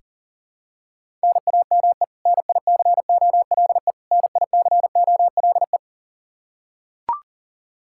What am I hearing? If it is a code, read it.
Morse code, heard as a keyed tone, about 40 wpm, 700 Hz. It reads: NAME NICOLE NICOLE
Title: Practice QSO - Exchange 1 - Beginning - 40wpm